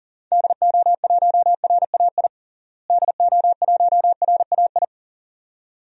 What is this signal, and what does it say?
Morse code, heard as a keyed tone, about 38 wpm, 700 Hz. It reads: DO1RAI DO1RAI
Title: DXCC-100 Call Signs - 6 Characters - Repeated 2x - 40wpm